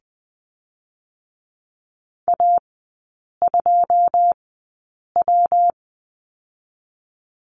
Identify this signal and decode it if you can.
Morse code, heard as a keyed tone, about 20 wpm, 700 Hz. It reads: A 2 W
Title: 114 Lesson 19e - Letter w - Three Characters 20wpm